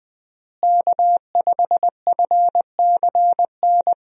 Morse code, heard as a keyed tone, about 21 wpm, 700 Hz. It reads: K5FCN
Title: USA Call Signs - 5 Characters - Repeated 3x - 20wpm